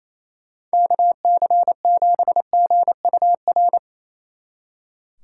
Morse code, heard as a keyed tone, about 35 wpm, 700 Hz. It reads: KC7GUR
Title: Speed Racing - USA 2x3 Call Signs 28wpm